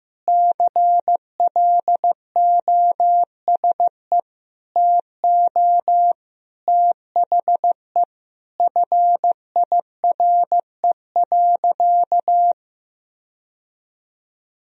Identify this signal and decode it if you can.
Morse code, heard as a keyed tone, about 15 wpm, 700 Hz. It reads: CLOSE TO THE FIRE.
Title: Sentences from Top 1000 Words - Review Part 1 - 15wpm